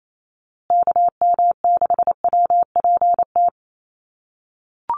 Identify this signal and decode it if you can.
Morse code, heard as a keyed tone, about 28 wpm, 700 Hz. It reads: KM6WPT
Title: USA Call Signs Repeated 3x 28wpm